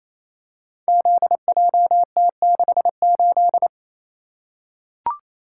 Morse code, heard as a keyed tone, about 29 wpm, 690 Hz. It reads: ZJT68
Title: Code Groups - 5 Letter:Numbers - 28wpm